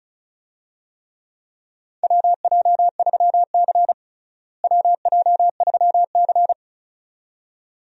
Morse code, heard as a keyed tone, about 36 wpm, 700 Hz. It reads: WJ3C WJ3C
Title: USA Call Signs - 4 Characters - Repeated 2x - 35wpm